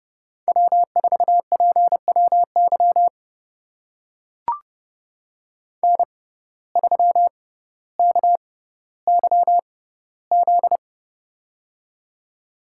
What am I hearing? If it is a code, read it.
Morse code, heard as a keyed tone, about 31 wpm, spaced as for 16 wpm, 700 Hz. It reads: W4PWY N3KYZ
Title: USA Call Signs - 5 Characters - 6x Character Spacing - 30wpm